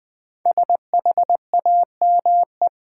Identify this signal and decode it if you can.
Morse code, heard as a keyed tone, about 21 wpm, 700 Hz. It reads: SHAME
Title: Five Letter Words - 20wpm